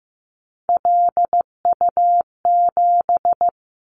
Morse code, heard as a keyed tone, about 15 wpm, 700 Hz. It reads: LU7
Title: Callsign Prefixes for Top 100 DXCC - 15wpm